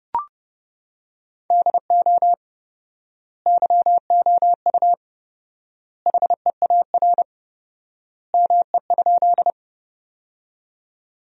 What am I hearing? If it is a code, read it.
Morse code, heard as a keyed tone, about 30 wpm, 700 Hz. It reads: DO YOU HEAR ME?
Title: 4x Word-Spacing - Sentences from Top 300 Words - 30wpm